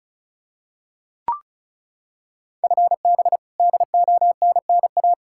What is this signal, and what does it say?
Morse code, heard as a keyed tone, about 34 wpm, 700 Hz. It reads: FB DONNA
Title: Practice QSO - Exchange 3 - More Info - 35wpm